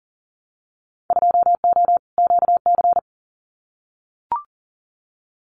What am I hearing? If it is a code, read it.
Morse code, heard as a keyed tone, about 40 wpm, 700 Hz. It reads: 2O QC
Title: Winter Field Day Exchanges 40wpm